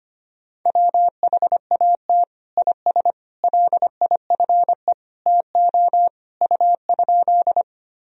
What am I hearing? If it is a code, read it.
Morse code, heard as a keyed tone, about 25 wpm, 700 Hz. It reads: WHAT IS LIFE TO U?
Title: Sentences from Top 300 Words and Abbreviations 25wpm